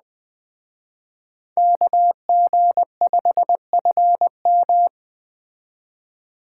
Morse code, heard as a keyed tone, about 21 wpm, 700 Hz. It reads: KG5FM